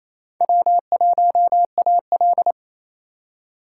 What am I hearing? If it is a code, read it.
Morse code, heard as a keyed tone, about 38 wpm, spaced as for 33 wpm, 700 Hz. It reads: W1AL
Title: Speed Racing - USA Call Signs 28wpm